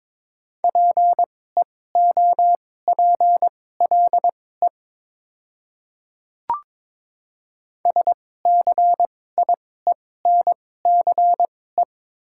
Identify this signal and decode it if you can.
Morse code, heard as a keyed tone, about 22 wpm, spaced as for 16 wpm, 700 Hz. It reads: PEOPLE SCIENCE
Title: 2x Character Spacing - Top 500 Words - 22wpm